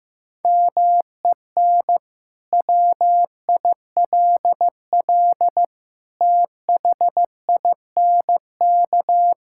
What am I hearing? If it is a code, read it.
Morse code, heard as a keyed tone, about 15 wpm, 700 Hz. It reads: MEN WILL THINK
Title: Sentences from Top 1000 Words - Review Part 5 - 15wpm